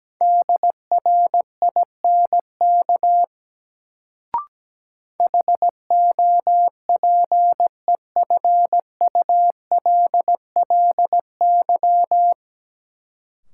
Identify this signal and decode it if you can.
Morse code, heard as a keyed tone, about 17 wpm, 700 Hz. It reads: DRINK HOPEFULLY
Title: Sentences from Top 1500 Words - New Word Drill Alt - 17wpm